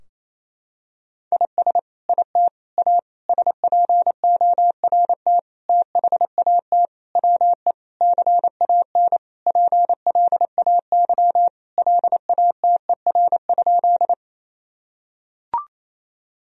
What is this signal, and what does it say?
Morse code, heard as a keyed tone, about 28 wpm, 700 Hz. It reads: IS IT A SPORT THAT WE CAN PLAY LATER?
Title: Sentences from Top 1000 Words - Review Part 4 - 28wpm